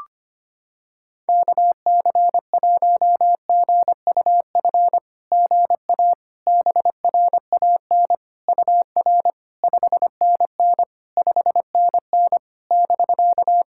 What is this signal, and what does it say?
Morse code, heard as a keyed tone, about 25 wpm, 700 Hz. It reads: KC1GUF GA BRAN UR 5NN 5NN <BK>
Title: SOTA QSO Practice 25wpm